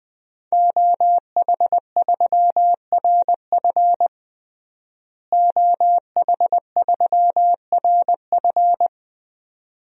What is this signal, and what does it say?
Morse code, heard as a keyed tone, about 21 wpm, 700 Hz. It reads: OH3RF OH3RF
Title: DXCC-100 Call Signs - 5 Characters - Repeated 2x - 20wpm